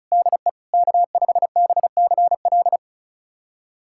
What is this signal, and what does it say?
Morse code, heard as a keyed tone, about 34 wpm, 700 Hz. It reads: DE K5BCL
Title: USA Callsign DE Callsign - 5 Characters - 35wpm